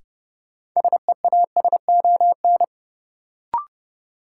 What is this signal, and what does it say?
Morse code, heard as a keyed tone, about 29 wpm, 700 Hz. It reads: SEASON
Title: Sentences from Top 1400 Words - New Word Drill Alt - 30wpm